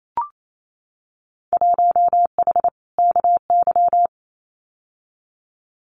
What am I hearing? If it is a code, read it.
Morse code, heard as a keyed tone, about 28 wpm, 700 Hz. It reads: 1H KY
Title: Winter Field Day Exchanges 28wpm